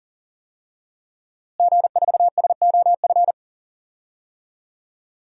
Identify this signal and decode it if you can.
Morse code, heard as a keyed tone, about 40 wpm, 700 Hz. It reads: G4SOF
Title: DXCC-100 Call Signs - 5 Characters - Repeated 2x - 40wpm